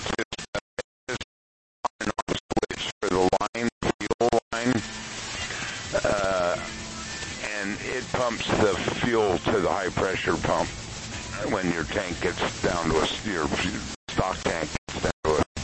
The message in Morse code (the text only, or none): none